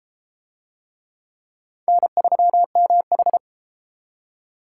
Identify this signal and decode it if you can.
Morse code, heard as a keyed tone, about 29 wpm, 700 Hz. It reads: N3MH